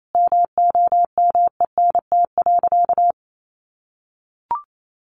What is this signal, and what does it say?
Morse code, heard as a keyed tone, about 28 wpm, 700 Hz. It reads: MOMENT.
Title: Sentences from Top 1100 Words 28wpm